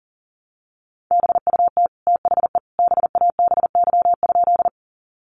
Morse code, heard as a keyed tone, about 37 wpm, 700 Hz. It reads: BUT THE BABY?
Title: Sentences from Top 1300 Words 40wpm